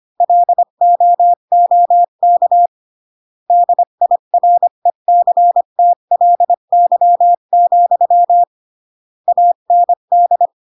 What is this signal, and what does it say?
Morse code, heard as a keyed tone, about 25 wpm, 700 Hz. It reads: LOOK DIRECTLY, AND